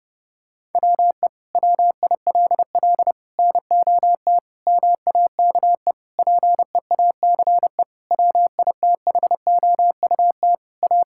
Morse code, heard as a keyed tone, about 30 wpm, 700 Hz. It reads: WE WILL NOT MAKE PEACE WITHOUT A
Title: Sentences from Top 1600 Words 30wpm